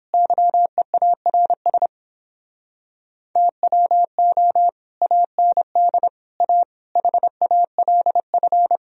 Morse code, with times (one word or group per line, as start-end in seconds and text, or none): 0.1-1.9 YEARS
3.4-4.7 TWO
5.0-6.1 AND
6.4-6.6 A
7.0-8.8 HALF